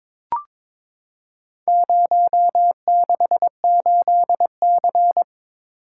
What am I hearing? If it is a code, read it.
Morse code, heard as a keyed tone, about 23 wpm, 700 Hz. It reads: 068C
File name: Four Letter-Number Combinations - 22wpm